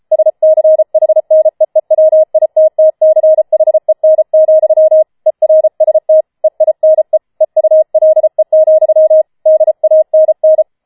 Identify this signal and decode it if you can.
Morse code, heard as a keyed tone, about 33 wpm, 630 Hz. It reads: SCHNEEWITTCHEN, ERST EINE EULE, DANN